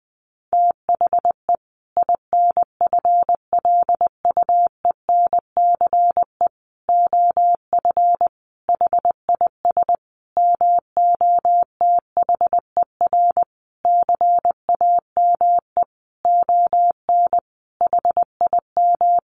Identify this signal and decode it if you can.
Morse code, heard as a keyed tone, about 20 wpm, 700 Hz. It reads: THE INFLUENCE OF HIS MOTHER CAME ON HIM
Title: Sentences from Top 1100 Words 20wpm